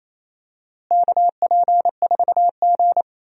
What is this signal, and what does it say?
Morse code, heard as a keyed tone, about 27 wpm, 700 Hz. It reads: KP4G
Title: DXCC-100 Call Signs - 4 Characters - Repeated 3x - 28wpm